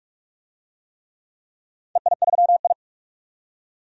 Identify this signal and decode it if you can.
Morse code, heard as a keyed tone, about 46 wpm, 700 Hz. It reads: EI3I
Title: DXCC-100 Call Signs - 4 Characters - 45wpm